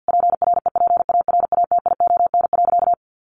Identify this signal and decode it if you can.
Morse code, heard as a keyed tone, about 50 wpm, 700 Hz. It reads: PREPARATION.
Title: Sentences from Top 1400 Words 50wpm